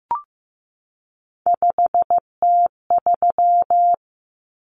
Morse code, heard as a keyed tone, about 15 wpm, 700 Hz. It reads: 5T3
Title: Common Cut Numbers - 3-Digit 15wpm